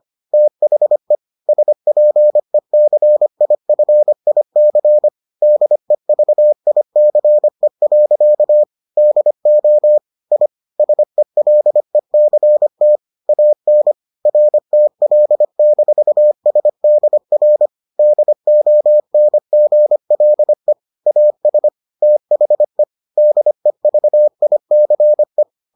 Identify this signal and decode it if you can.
Morse code, heard as a keyed tone, about 25 wpm, 600 Hz. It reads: THE SPECIFIC DEVICE. DO I SELECT AN RTL-SDR DONGLE AS THE DEVICE